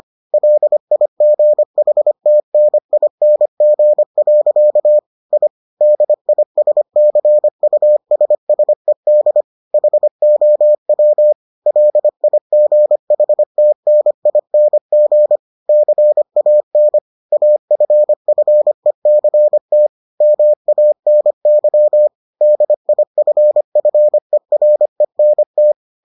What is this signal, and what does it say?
Morse code, heard as a keyed tone, about 25 wpm, 600 Hz. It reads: LIGHTNING. I DISCUSSED HOW LIGHTNING CAN AFFECT MANY DIFFERENT